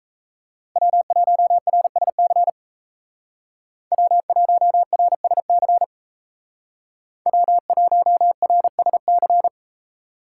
Morse code, heard as a keyed tone, about 35 wpm, 700 Hz. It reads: W1RSC W1RSC W1RSC